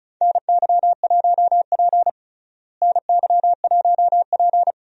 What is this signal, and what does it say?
Morse code, heard as a keyed tone, about 36 wpm, 700 Hz. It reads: NY1P NY1P
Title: USA Call Signs - 4 Characters - Repeated 2x - 35wpm